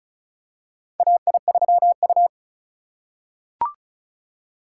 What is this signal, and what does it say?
Morse code, heard as a keyed tone, about 34 wpm, 700 Hz. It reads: AI3U
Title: USA Call Signs - 4 Characters - 4x Character Spacing - 35wpm